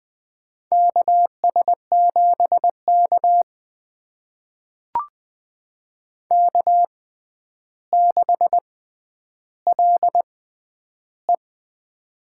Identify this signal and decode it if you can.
Morse code, heard as a keyed tone, about 20 wpm, spaced as for 10 wpm, 700 Hz. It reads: KS7K K6LE
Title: USA Call Signs - 4 Characters - 6x Character Spacing - 20wpm